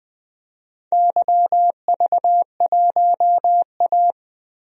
Morse code, heard as a keyed tone, about 20 wpm, 700 Hz. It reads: YV1A
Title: DXCC-100 Call Signs - 4 Characters - Repeated 2x - 20wpm